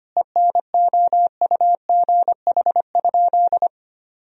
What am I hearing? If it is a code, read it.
Morse code, heard as a keyed tone, about 25 wpm, 700 Hz. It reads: ENOUGH?
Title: Sentences from Top 1000 Words - Review Part 2 - 25wpm